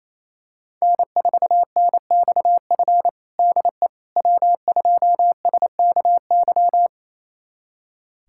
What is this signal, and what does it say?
Morse code, heard as a keyed tone, about 28 wpm, 700 Hz. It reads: N4NXF DE W2SKY
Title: USA Callsign DE Callsign - 5 Characters - 28wpm